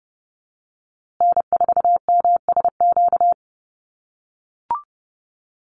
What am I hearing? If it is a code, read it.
Morse code, heard as a keyed tone, about 30 wpm, 700 Hz. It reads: N4MSQ
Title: USA Call Signs Repeated 2x 30wpm